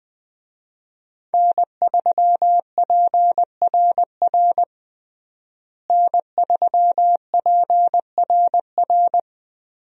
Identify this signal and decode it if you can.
Morse code, heard as a keyed tone, about 20 wpm, 700 Hz. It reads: N3PRR N3PRR
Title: USA Call Signs - 5 Characters - Repeated 3x - 20wpm